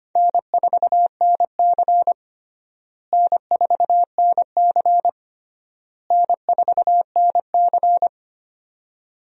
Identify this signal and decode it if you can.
Morse code, heard as a keyed tone, about 25 wpm, 700 Hz. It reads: N4NC N4NC N4NC
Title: USA Call Signs - 4 Characters - Repeated 3x - 25wpm